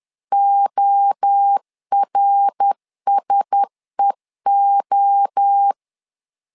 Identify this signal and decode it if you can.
Morse code, heard as a keyed tone, about 10 wpm, 800 Hz. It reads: ORSEO
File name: LEZIONI / 4 Q1S 10wpm 2133